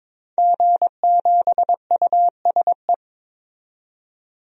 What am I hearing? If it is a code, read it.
Morse code, heard as a keyed tone, about 22 wpm, 700 Hz. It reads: G7USE
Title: DXCC-100 Call Signs - 5 Characters - 22wpm